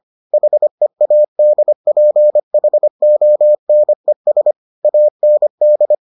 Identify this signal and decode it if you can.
Morse code, heard as a keyed tone, about 25 wpm, 600 Hz. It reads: HEADPHONES AND